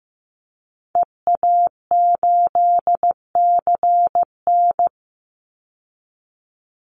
Morse code, heard as a keyed tone, about 15 wpm, 700 Hz. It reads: EA8CN
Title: Call Signs Repeated 3x - 15wpm